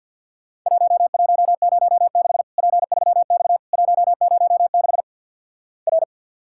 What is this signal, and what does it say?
Morse code, heard as a keyed tone, about 48 wpm, 700 Hz. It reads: 1106 P3X 106 R
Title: Worked All Europe DX Contest - QTC Exchange with Repeats - 50wpm